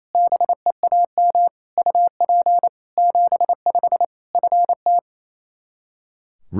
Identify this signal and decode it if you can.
Morse code, heard as a keyed tone, about 28 wpm, 700 Hz. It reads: BEAM UP 75 FT
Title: Practice QSO - Exchange 2 - Info - 28wpm